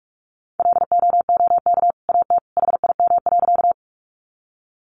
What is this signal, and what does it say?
Morse code, heard as a keyed tone, about 44 wpm, 700 Hz. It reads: LOOK AT HIM.
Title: Sentences from Top 100 Words 45wpm